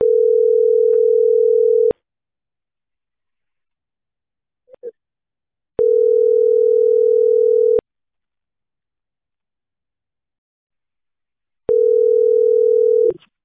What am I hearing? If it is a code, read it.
Ringback tone (North America), 440 + 480 Hz.